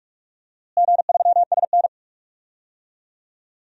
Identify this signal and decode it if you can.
Morse code, heard as a keyed tone, about 46 wpm, 700 Hz. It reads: G3SN
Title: DXCC-100 Call Signs - 4 Characters - Repeated 2x - 45wpm